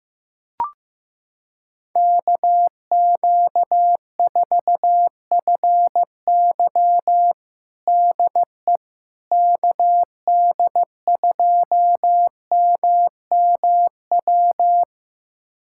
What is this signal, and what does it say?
Morse code, heard as a keyed tone, about 16 wpm, 700 Hz. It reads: KQ4FY DE KD2MMW
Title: USA Callsign DE Callsign 15wpm